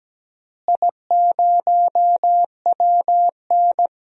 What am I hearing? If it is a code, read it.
Morse code, heard as a keyed tone, about 17 wpm, 700 Hz. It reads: I0WN
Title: DXCC-100 Call Signs - 4 Characters - Repeated 3x - 17wpm